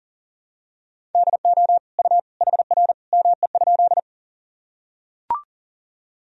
Morse code, heard as a keyed tone, about 40 wpm, 700 Hz. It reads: DO U HR ME?
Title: Sentences from Top 300 Words and Abbreviations 40wpm